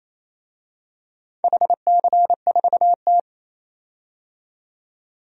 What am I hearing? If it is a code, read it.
Morse code, heard as a keyed tone, about 28 wpm, 700 Hz. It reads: HC4T
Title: DXCC-100 Call Signs - 4 Characters - 28wpm